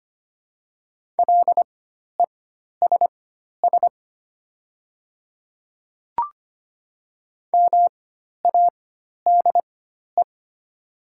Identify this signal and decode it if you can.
Morse code, heard as a keyed tone, about 25 wpm, spaced as for 12 wpm, 700 Hz. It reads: LESS MADE
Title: 4x Character Spacing - Top 500 Words - 25wpm